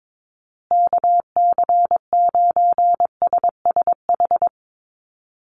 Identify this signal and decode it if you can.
Morse code, heard as a keyed tone, about 22 wpm, 700 Hz. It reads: KC9SSH
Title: USA Call Signs Repeated 2x 22wpm